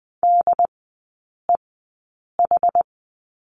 Morse code, heard as a keyed tone, about 20 wpm, spaced as for 8 wpm, 700 Hz. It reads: DEH
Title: 067 Lesson 12e - Letter d - Three Characters 20wpm